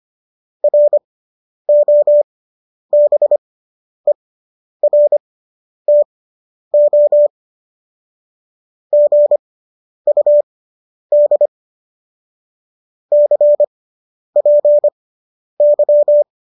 Morse code, heard as a keyed tone, about 25 wpm, spaced as for 10 wpm, 590 Hz. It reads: ROBERTO GUD CPY